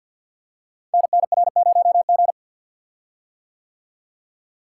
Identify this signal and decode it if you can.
Morse code, heard as a keyed tone, about 50 wpm, 700 Hz. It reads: NNR0G
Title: Code Groups - 5 Letter:Numbers - 50wpm